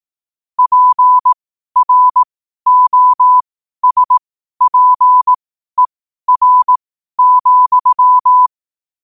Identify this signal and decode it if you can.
Morse code, heard as a keyed tone, about 18 wpm, spaced as for 12 wpm, 1000 Hz. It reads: PROSPER,
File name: qotdfiles / 12wpm